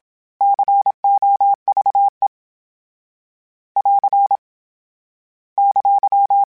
Morse code, heard as a keyed tone, about 27 wpm, 800 Hz. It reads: COVE <AR> !
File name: fromSMs / FromSMs mp3s / Cities@25